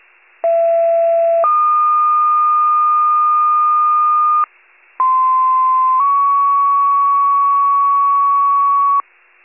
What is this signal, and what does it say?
Two-tone page, heard as a one-second tone, then a three-second tone, about 670 then 1220 Hz, about 1040 then 1120 Hz. It is heard from 0.4 to 9.0 s.